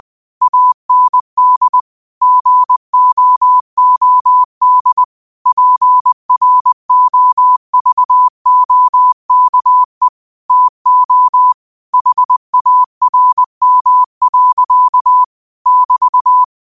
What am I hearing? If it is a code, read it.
Morse code, heard as a keyed tone, about 20 wpm, 1000 Hz. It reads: AND GOOD PROVOKE TO HARM. <BT>